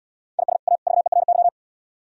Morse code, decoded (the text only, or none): IS A VERY